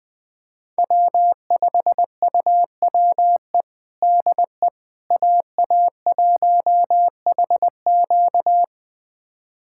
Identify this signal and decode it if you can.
Morse code, heard as a keyed tone, about 20 wpm, 700 Hz. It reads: W5UWE DE AA1HQ
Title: USA Callsign DE Callsign - 5 Characters - 20wpm